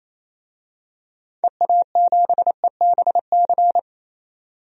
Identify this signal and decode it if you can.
Morse code, heard as a keyed tone, about 28 wpm, 700 Hz. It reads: EA7EBC
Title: DXCC-100 Call Signs - 6 Characters - 28wpm